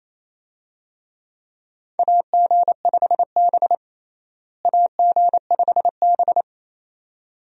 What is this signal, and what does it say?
Morse code, heard as a keyed tone, about 28 wpm, 700 Hz. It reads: AG5B AG5B
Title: USA Call Signs - 4 Characters - Repeated 2x - 28wpm